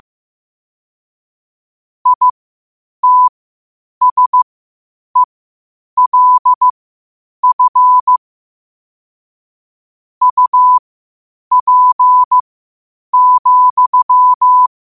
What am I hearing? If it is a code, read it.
Morse code, heard as a keyed tone, about 15 wpm, spaced as for 8 wpm, 1000 Hz. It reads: ITSELF UP,
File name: qotdfiles / 8wpm